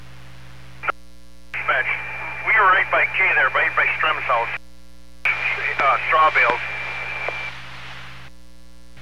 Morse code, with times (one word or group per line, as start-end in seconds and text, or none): none